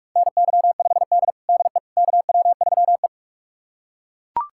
Morse code, heard as a keyed tone, about 42 wpm, 700 Hz. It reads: NY5D DE KW3E
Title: USA Callsign DE Callsign - 4 Characters - 45wpm